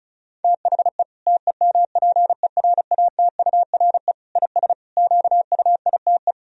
Morse code, heard as a keyed tone, about 36 wpm, 700 Hz. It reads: THE TEMPERATURE IS QUITE
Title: Sentences from Top 1000 Words - Review Part 4 - 35wpm